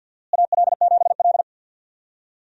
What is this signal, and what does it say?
Morse code, heard as a keyed tone, about 49 wpm, 700 Hz. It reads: AL7L
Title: USA Call Signs - 4 Characters - Repeated 3x - 50wpm